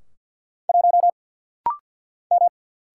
Morse code, heard as a keyed tone, about 50 wpm, 690 Hz. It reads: 1 M
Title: Single Letter-Number - Rapid-Fire 50wpm